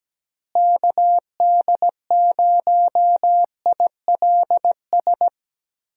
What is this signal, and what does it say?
Morse code, heard as a keyed tone, about 17 wpm, 700 Hz. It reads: KD0ILS
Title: USA Call Signs - 6 Characters - Repeated 3x - 17wpm